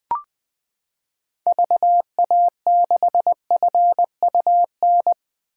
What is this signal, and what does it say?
Morse code, heard as a keyed tone, about 20 wpm, 700 Hz. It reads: VA6FUN
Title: 2023 SOTA Chaser Call Signs - 20wpm